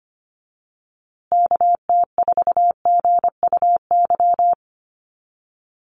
Morse code, heard as a keyed tone, about 25 wpm, 700 Hz. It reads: KT4GUY
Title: USA Call Signs Repeated 2x 25wpm